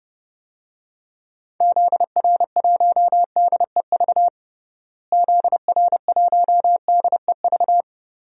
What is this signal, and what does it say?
Morse code, heard as a keyed tone, about 30 wpm, 700 Hz. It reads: ZR1DEV ZR1DEV
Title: DXCC-100 Call Signs - 6 Characters - Repeated 3x - 30wpm